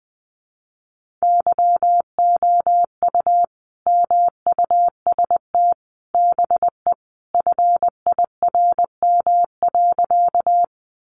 Morse code, heard as a keyed tone, about 20 wpm, 700 Hz. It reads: YOU MUST BE FIRM.
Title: Sentences from Top 1400 Words 20wpm